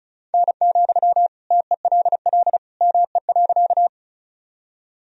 Morse code, heard as a keyed tone, about 36 wpm, 700 Hz. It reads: N, TELL ME.
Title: Sentences from Top 200 Words and Abbreviations 35wpm